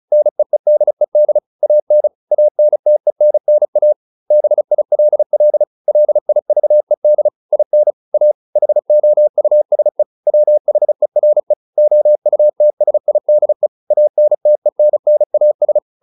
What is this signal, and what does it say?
Morse code, heard as a keyed tone, about 34 wpm, 600 Hz. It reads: NEEDED AN ANTENNA BILL LIVED IN A HOUSE WHERE OUTSIDE ANTENNAS